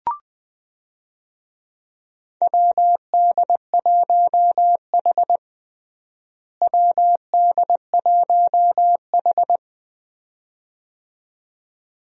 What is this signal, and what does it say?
Morse code, heard as a keyed tone, about 21 wpm, 700 Hz. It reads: WD1H WD1H
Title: USA Call Signs - 4 Characters - Repeated 2x - 20wpm